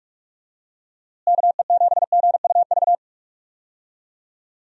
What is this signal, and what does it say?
Morse code, heard as a keyed tone, about 46 wpm, 700 Hz. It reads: KE7GUV
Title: USA Call Signs - 6 Characters - Repeated 2x - 45wpm